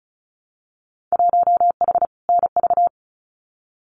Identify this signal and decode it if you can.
Morse code, heard as a keyed tone, about 34 wpm, 700 Hz. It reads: 1H NV
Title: Winter Field Day Exchanges 35wpm